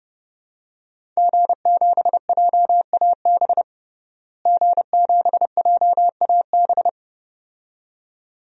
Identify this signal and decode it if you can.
Morse code, heard as a keyed tone, about 30 wpm, 700 Hz. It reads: G7JAB G7JAB
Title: DXCC-100 Call Signs - 5 Characters - Repeated 2x - 30wpm